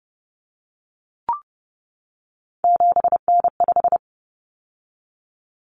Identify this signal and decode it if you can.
Morse code, heard as a keyed tone, about 30 wpm, 700 Hz. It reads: 7N5
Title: Common Cut Numbers - 3-Digit 30wpm